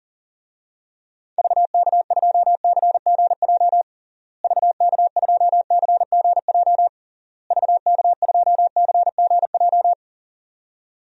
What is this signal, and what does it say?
Morse code, heard as a keyed tone, about 37 wpm, 700 Hz. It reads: VK2CGJ VK2CGJ VK2CGJ
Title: DXCC-100 Call Signs - 6 Characters - Repeated 3x - 40wpm